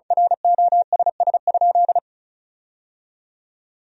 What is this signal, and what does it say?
Morse code, heard as a keyed tone, about 36 wpm, 700 Hz. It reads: ROSS?
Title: 2x Word-Spacing - Sentences from Top 300 Words - 35wpm